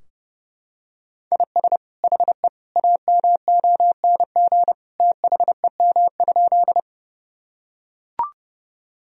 Morse code, heard as a keyed tone, about 30 wpm, 700 Hz. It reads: IS HE AMONG THEM?
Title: Sentences from Top 1000 Words - Review Part 3 - 30wpm